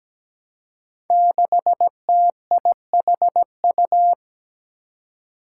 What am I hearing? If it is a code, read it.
Morse code, heard as a keyed tone, about 17 wpm, 700 Hz. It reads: 6TIHU
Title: Five Challenging Letter-Number Combinations 17wpm